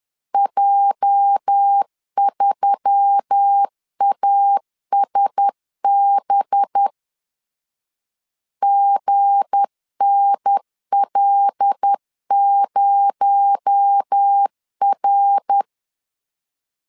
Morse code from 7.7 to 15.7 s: GNL0R